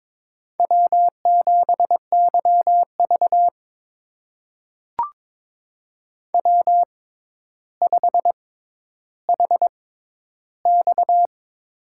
Morse code, heard as a keyed tone, about 22 wpm, spaced as for 11 wpm, 700 Hz. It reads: W7YV W5HX
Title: USA Call Signs - 4 Characters - 6x Character Spacing - 22wpm